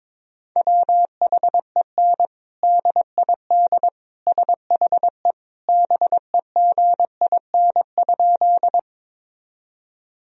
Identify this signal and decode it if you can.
Morse code, heard as a keyed tone, about 22 wpm, 700 Hz. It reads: WHEN DID SHE BEGIN?
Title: Sentences from Top 1000 Words - Review Part 1 - 22wpm